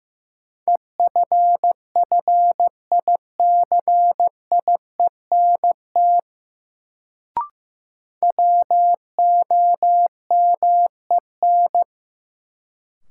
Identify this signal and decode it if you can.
Morse code, heard as a keyed tone, about 16 wpm, 700 Hz. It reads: EFFICIENT WOMEN
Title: Sentences from Top 1300 Words - New Word Drill Alt - 15wpm